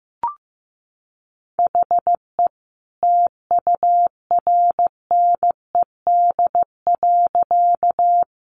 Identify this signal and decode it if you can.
Morse code, heard as a keyed tone, about 15 wpm, 700 Hz. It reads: HE TURNED.
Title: A Sound Of Thunder 15wpm